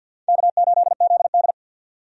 Morse code, heard as a keyed tone, about 52 wpm, 700 Hz. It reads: K8ZD
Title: USA Call Signs - 4 Characters - 50wpm